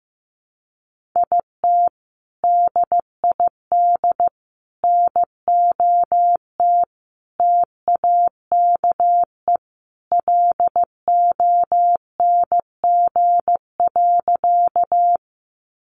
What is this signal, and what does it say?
Morse code, heard as a keyed tone, about 15 wpm, 700 Hz. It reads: IT DID NOT TAKE LONG.
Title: Sentences from Top 600 Words 15wpm